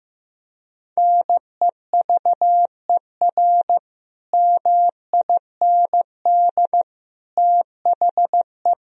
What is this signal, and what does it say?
Morse code, heard as a keyed tone, about 15 wpm, 700 Hz. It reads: NEVER MIND THE
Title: Sentences from Top 1000 Words - Review Part 4 - 15wpm